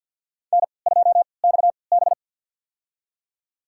Morse code, heard as a keyed tone, about 53 wpm, spaced as for 29 wpm, 700 Hz. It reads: N2XB
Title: USA Call Signs - 3x Character Spacing - 50wpm